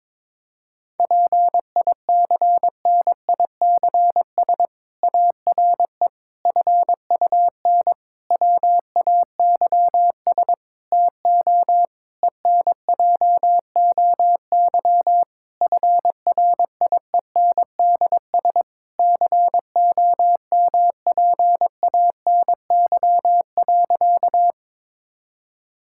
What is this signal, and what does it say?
Morse code, heard as a keyed tone, about 22 wpm, 700 Hz. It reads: PICNICS ARE FUN WAYS TO ENJOY FRIENDS COMPANY.
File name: Difficult Sentences with Hints - 22wpm